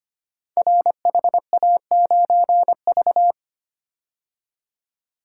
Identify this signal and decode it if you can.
Morse code, heard as a keyed tone, about 25 wpm, 700 Hz. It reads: RHA9V
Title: Code Groups - 5 Letter:Numbers - 25wpm